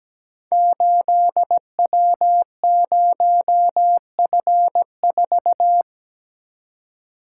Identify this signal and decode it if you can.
Morse code, heard as a keyed tone, about 17 wpm, 700 Hz. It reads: 8W0F4
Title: Code Groups - 5 Letter:Numbers - 17wpm